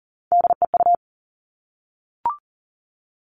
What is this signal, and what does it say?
Morse code, heard as a keyed tone, about 40 wpm, 700 Hz. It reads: DEU
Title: All Cut Numbers - 3-Digit 40wpm